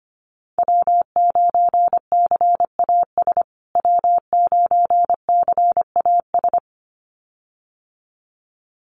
Morse code, heard as a keyed tone, about 25 wpm, 700 Hz. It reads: W9CAS W9CAS
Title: USA Call Signs Repeated 2x 25wpm